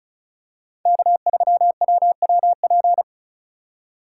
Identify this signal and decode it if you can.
Morse code, heard as a keyed tone, about 34 wpm, 700 Hz. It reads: K3WWP
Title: K1USN SST QSO Practice - 35wpm